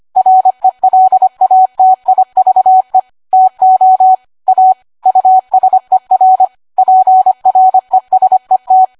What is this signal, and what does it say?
Morse code, heard as a keyed tone, about 25 wpm, 750 Hz. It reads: RELATIVE TO A USER PRESET